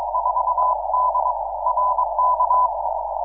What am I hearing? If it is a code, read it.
Morse code, heard as a keyed tone, about 22 wpm, 1000 Hz, with noise. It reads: 4D RK
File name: iw0ffk / 2006 06 16 W4DR IW0FFK